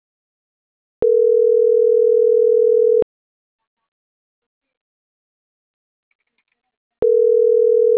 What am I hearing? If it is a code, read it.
Ringback tone (North America), 440 + 480 Hz.